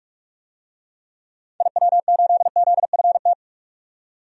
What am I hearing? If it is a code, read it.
Morse code, heard as a keyed tone, about 46 wpm, 700 Hz. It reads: IW8ZFT
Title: DXCC-100 Call Signs - 6 Characters - 45wpm